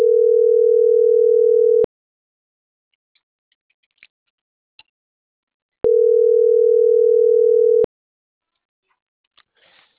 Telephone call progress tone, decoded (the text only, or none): ringback tone